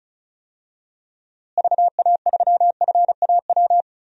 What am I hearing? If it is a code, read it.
Morse code, heard as a keyed tone, about 34 wpm, 700 Hz. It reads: VA3FAW
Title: DXCC-100 Call Signs - 6 Characters - Repeated 3x - 35wpm